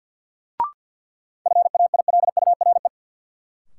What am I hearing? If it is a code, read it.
Morse code, heard as a keyed tone, about 52 wpm, 700 Hz. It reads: FAILURE
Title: Sentences from Top 900 Words - New Word Drill - 50wpm